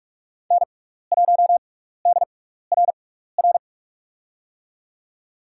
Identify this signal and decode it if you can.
Morse code, heard as a keyed tone, about 45 wpm, spaced as for 15 wpm, 700 Hz. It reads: N1DRR
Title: USA Call Signs - 5 Characters - 6x Character Spacing - 45wpm